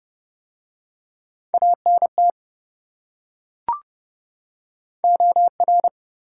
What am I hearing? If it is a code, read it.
Morse code, heard as a keyed tone, about 25 wpm, 700 Hz. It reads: ANT OR
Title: Speed-Racing Top 180 QSO Elements 20wpm